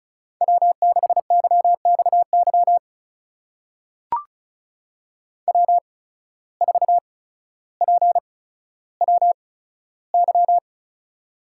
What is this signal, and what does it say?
Morse code, heard as a keyed tone, about 36 wpm, spaced as for 15 wpm, 700 Hz. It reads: W6YXY W4PWY
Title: USA Call Signs - 5 Characters - 8x Character Spacing - 35wpm